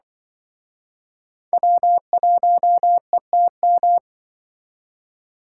Morse code, heard as a keyed tone, about 25 wpm, 700 Hz. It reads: W1ETM
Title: Speed Racing - USA 1x3 Call Signs 20wpm